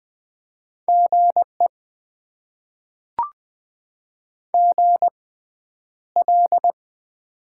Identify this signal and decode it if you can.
Morse code, heard as a keyed tone, about 20 wpm, spaced as for 10 wpm, 700 Hz. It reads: GE GL
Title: Top 90 QSO Elements - 6x Character Spacing - 20wpm